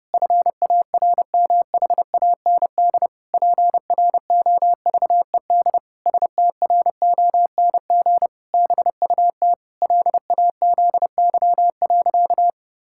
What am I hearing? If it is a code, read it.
Morse code, heard as a keyed tone, about 30 wpm, 700 Hz. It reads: FARMHAND PROVED STRONG BUT LAZY.
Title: Pangram Challenge v2 - 30wpm